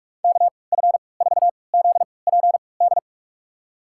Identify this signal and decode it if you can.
Morse code, heard as a keyed tone, about 46 wpm, spaced as for 25 wpm, 700 Hz. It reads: KF4ZPD
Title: USA Call Signs - 3x Character Spacing - 45wpm